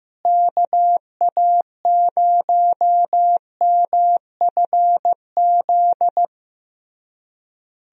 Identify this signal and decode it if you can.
Morse code, heard as a keyed tone, about 15 wpm, 700 Hz. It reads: KA0MFZ
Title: USA Call Signs - 6 Characters - Repeated 3x - 15wpm